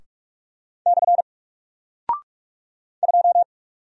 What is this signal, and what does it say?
Morse code, heard as a keyed tone, about 46 wpm, 700 Hz. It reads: / 2
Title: Single Character (Letter:Number:Punctuation:Prosign) Alt 45wpm